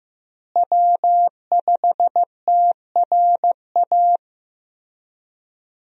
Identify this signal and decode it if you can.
Morse code, heard as a keyed tone, about 15 wpm, 700 Hz. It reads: W5TRA
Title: USA Call Signs - 5 Characters - Repeated 2x - 15wpm